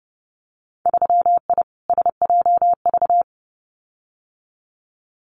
Morse code, heard as a keyed tone, about 30 wpm, 700 Hz. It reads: 3I SJV
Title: Winter Field Day Exchanges Encore 30wpm